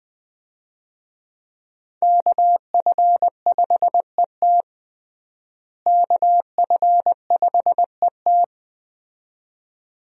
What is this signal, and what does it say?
Morse code, heard as a keyed tone, about 20 wpm, 700 Hz. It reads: KF5ET KF5ET
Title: USA Call Signs - 5 Characters - Repeated 2x - 20wpm